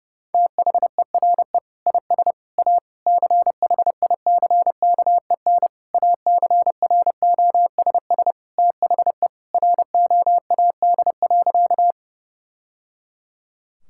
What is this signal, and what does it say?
Morse code, heard as a keyed tone, about 30 wpm, 700 Hz. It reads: THERE IS A CHICKEN ACROSS THE ROAD.
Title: Sentences from Top 1000 Words - Review Part 5 - 30wpm